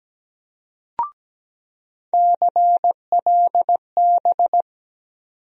Morse code, heard as a keyed tone, about 17 wpm, 700 Hz. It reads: CLB
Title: USA Call Sign Postfixes - 3 Letters 17wpm